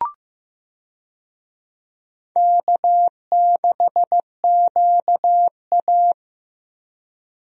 Morse code, heard as a keyed tone, about 15 wpm, 700 Hz. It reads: K6QA
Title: USA Call Signs - 4 Characters - Repeated 2x - 15wpm